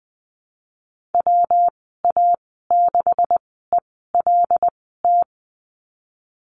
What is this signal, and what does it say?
Morse code, heard as a keyed tone, about 20 wpm, spaced as for 15 wpm, 700 Hz. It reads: WA6ELT
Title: 167 Lesson 27g - Number 7 - Call Signs 20wpm